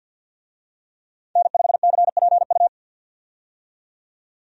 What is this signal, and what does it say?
Morse code, heard as a keyed tone, about 50 wpm, 700 Hz. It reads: N5CPU
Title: USA Call Signs - 5 Characters - 50wpm